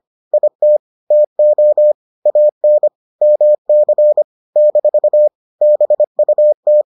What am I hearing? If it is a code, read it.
Morse code, heard as a keyed tone, about 25 wpm, 600 Hz. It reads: IT TO AN MC - BUT